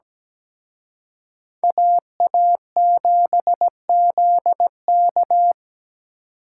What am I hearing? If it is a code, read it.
Morse code, heard as a keyed tone, about 20 wpm, 700 Hz. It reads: AA7ZK
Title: Speed Racing - USA Call Signs 17wpm